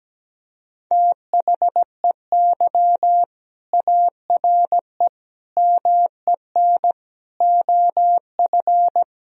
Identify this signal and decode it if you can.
Morse code, heard as a keyed tone, about 17 wpm, 700 Hz. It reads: THEY ARE MEN OF
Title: Sentences from Top 1000 Words - Review Part 2 - 17wpm